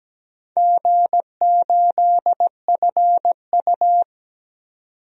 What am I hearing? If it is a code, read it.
Morse code, heard as a keyed tone, about 17 wpm, 700 Hz. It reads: G8FU
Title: DXCC-100 Call Signs - 4 Characters - 17wpm